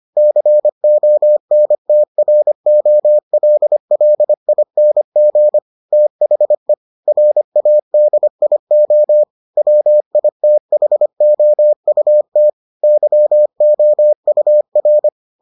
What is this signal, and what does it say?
Morse code, heard as a keyed tone, about 25 wpm, 600 Hz. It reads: CONTROLLING THE RADIO WITHOUT YOUR